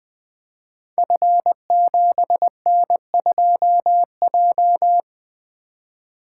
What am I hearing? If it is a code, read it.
Morse code, heard as a keyed tone, about 20 wpm, 700 Hz. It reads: F7N2J
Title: Code Groups - 5 Letter:Numbers - 20wpm